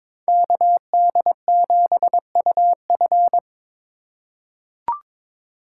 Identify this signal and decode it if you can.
Morse code, heard as a keyed tone, about 22 wpm, 700 Hz. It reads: KD7UF
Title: USA Call Signs - 2x Character Spacing - 22wpm